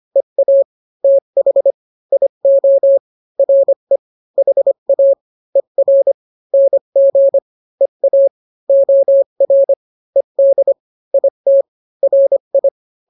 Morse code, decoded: EA TH IO RE HA ER NG EA OR ED IT RI